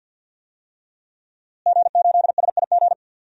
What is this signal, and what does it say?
Morse code, heard as a keyed tone, about 51 wpm, 700 Hz. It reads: G8SIG
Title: DXCC-100 Call Signs - 5 Characters - Repeated 3x - 50wpm